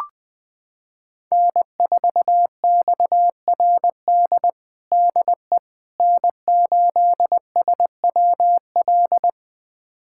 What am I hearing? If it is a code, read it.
Morse code, heard as a keyed tone, about 21 wpm, 700 Hz. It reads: N4XRD DE N8SWL
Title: USA Callsign DE Callsign 20wpm